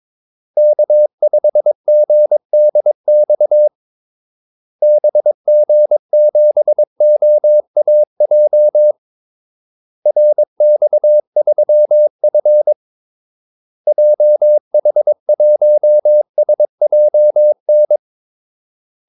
Morse code, heard as a keyed tone, about 22 wpm, 600 Hz. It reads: K5GDX BG7OAJ RX3F JH1SJN
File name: Callsigns-22wpm